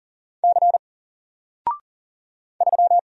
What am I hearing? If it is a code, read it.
Morse code, heard as a keyed tone, about 40 wpm, 700 Hz. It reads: C 3
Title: Single Letter-Number 40wpm